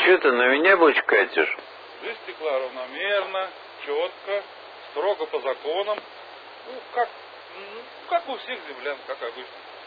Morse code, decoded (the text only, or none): none